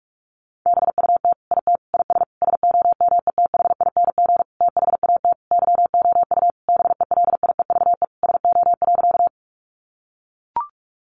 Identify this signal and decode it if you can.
Morse code, heard as a keyed tone, about 44 wpm, 700 Hz. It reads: BUT IT IS SOMETHING THAT YOU BELIEVE SO.
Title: Sentences from Top 1000 Words 45wpm